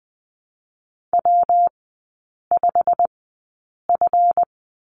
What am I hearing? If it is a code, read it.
Morse code, heard as a keyed tone, about 20 wpm, spaced as for 8 wpm, 700 Hz. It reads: W5F
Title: 138 Lesson 23e - Letter f - Three Characters 20wpm